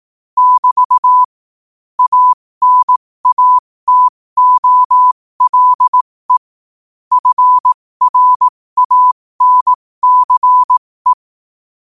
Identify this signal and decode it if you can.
Morse code, heard as a keyed tone, about 18 wpm, spaced as for 15 wpm, 1000 Hz. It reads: <BT> ANATOLE FRANCE
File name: qotdfiles / 15wpm